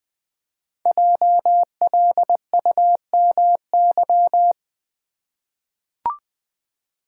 Morse code, heard as a keyed tone, about 20 wpm, 700 Hz. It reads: JLUMY
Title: Code Groups - 5 Letters - 20wpm